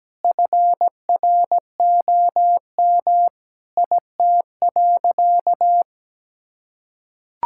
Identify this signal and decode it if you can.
Morse code, heard as a keyed tone, about 17 wpm, 700 Hz. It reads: FROM IT.
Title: Sentences from Top 1000 Words - Review Part 6 - 17wpm